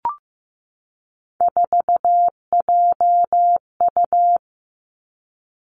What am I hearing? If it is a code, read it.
Morse code, heard as a keyed tone, about 15 wpm, 700 Hz. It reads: 4JU
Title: Three Challenging Letter-Number Combinations 15wpm